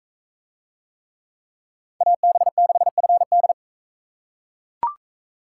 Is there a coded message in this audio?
Yes: Morse code from 2.0 to 3.5 s, heard as a keyed tone, about 44 wpm, 700 Hz. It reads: AB6FD